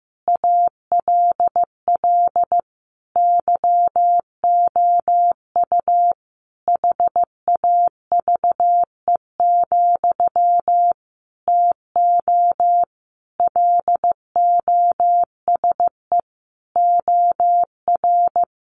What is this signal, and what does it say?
Morse code, heard as a keyed tone, about 15 wpm, 700 Hz. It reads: ALL YOU HAVE, TO LOSE OR
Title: Sentences from Top 1100 Words 15wpm